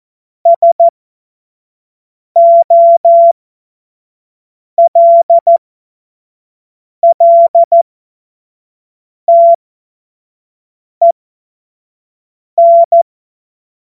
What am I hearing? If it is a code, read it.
Morse code, heard as a keyed tone, about 14 wpm, spaced as for 5 wpm, 680 Hz. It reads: SOLLTEN
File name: Morsekurs / Witze / Witz 12 März 2024 25 F70